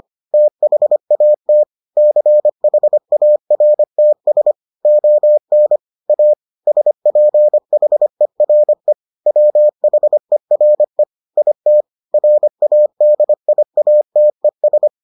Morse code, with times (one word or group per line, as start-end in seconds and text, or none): 0.3-1.6 THAT
2.0-4.5 CHARTS
4.9-5.8 ON
6.1-6.3 A
6.7-8.9 SPHERE
9.3-11.0 WHERE
11.4-11.8 IT
12.1-14.9 RADIATES